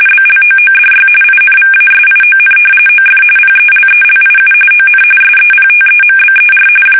RTTY, about 45.45 baud. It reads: BLOODY INVASION OF PELELIU ON SEPTEMBER
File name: fsk850 / battleshipnj